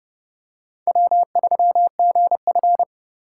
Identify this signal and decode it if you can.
Morse code, heard as a keyed tone, about 30 wpm, 700 Hz. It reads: W3GF
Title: USA Call Signs - 4 Characters - 3x Character Spacing - 30wpm